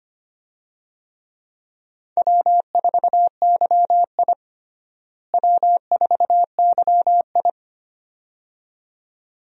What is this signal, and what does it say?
Morse code, heard as a keyed tone, about 25 wpm, 700 Hz. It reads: W4YI W4YI
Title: USA Call Signs - 4 Characters - Repeated 2x - 25wpm